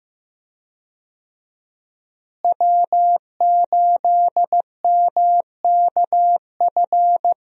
Morse code, heard as a keyed tone, about 16 wpm, 700 Hz. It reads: W8MKF
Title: USA Call Signs - 5 Characters - Repeated 2x - 15wpm